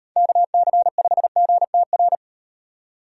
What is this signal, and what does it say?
Morse code, heard as a keyed tone, about 38 wpm, 700 Hz. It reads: KC5GTR